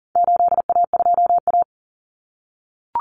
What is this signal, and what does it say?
Morse code, heard as a keyed tone, about 40 wpm, 700 Hz. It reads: 8A2A
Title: Common Cut Numbers - 4-Digit 40wpm